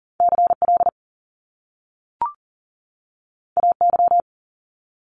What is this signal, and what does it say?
Morse code, heard as a keyed tone, about 40 wpm, 700 Hz. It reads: CL AY
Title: Top Two Letter Combinations in Words 40wpm